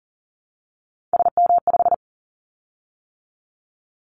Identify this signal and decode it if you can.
Morse code, heard as a keyed tone, about 40 wpm, 700 Hz. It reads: SM5
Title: Callsign Prefixes for Top 25 DXCC - 40wpm